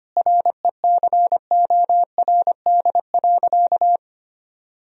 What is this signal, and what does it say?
Morse code, heard as a keyed tone, about 25 wpm, 700 Hz. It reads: RECORD.
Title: Sentences from Top 1000 Words - Review Part 3 - 25wpm